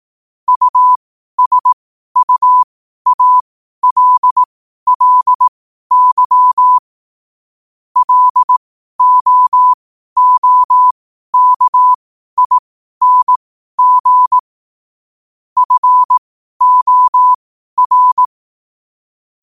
Morse code, heard as a keyed tone, about 18 wpm, spaced as for 12 wpm, 1000 Hz. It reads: USUALLY LOOKING FOR
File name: qotdfiles / 12wpm